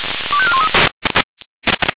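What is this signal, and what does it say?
Five-tone selective call (ZVEI), heard as quick runs of five tones, tones of about 70 ms. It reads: 25413
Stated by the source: ZVEI five-tone selective call received on SDR.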